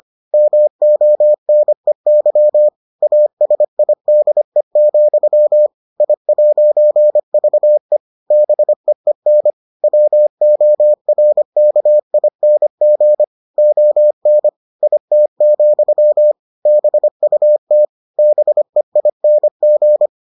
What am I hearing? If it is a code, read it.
Morse code, heard as a keyed tone, about 25 wpm, 600 Hz. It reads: MONEY ASIDE, I'VE BEEN WORKING ON IT, BUT BEING